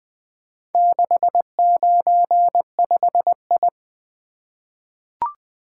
Morse code, heard as a keyed tone, about 20 wpm, 700 Hz. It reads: 695I
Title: Four Letter-Number Combinations - 20wpm